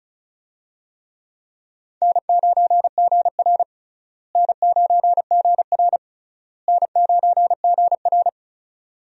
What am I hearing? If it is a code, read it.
Morse code, heard as a keyed tone, about 36 wpm, 700 Hz. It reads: N9GR N9GR N9GR
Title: USA Call Signs - 4 Characters - Repeated 3x - 35wpm